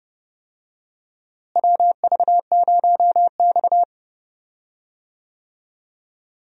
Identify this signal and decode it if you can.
Morse code, heard as a keyed tone, about 30 wpm, 700 Hz. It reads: WV0X
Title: USA Call Signs - 4 Characters - Repeated 2x - 30wpm